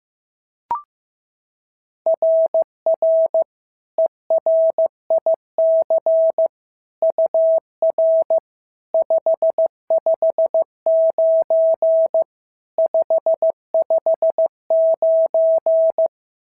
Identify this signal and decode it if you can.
Morse code, heard as a keyed tone, about 14 wpm, 650 Hz. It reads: RR ERIC UR 559 559
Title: SOTA QSO Practice 15wpm